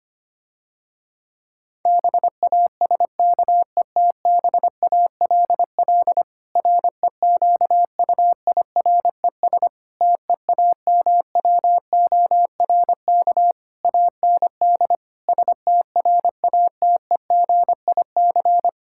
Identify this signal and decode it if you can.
Morse code, heard as a keyed tone, about 25 wpm, 700 Hz. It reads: BASKETBALL REQUIRES TEAMWORK AND STRATEGIC
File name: practice / 25wpm / Difficult Sentences with Hints - 25wpm